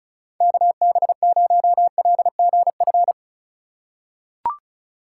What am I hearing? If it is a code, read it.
Morse code, heard as a keyed tone, about 36 wpm, 700 Hz. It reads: KB0LGF
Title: K1USN SST QSO Practice - 35wpm